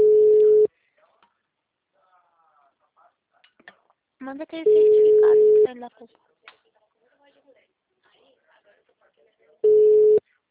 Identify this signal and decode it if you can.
Ringback tone (Europe, 425 Hz), with speech.